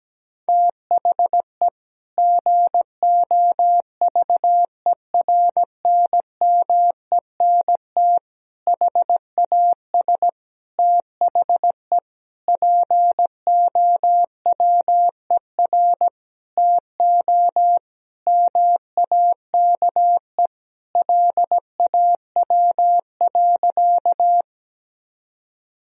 Morse code, heard as a keyed tone, about 17 wpm, 700 Hz. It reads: THE GOVERNMENT HAS THE POWER TO MAKE LAW.
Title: Sentences from Top 1000 Words - Review Part 3 - 17wpm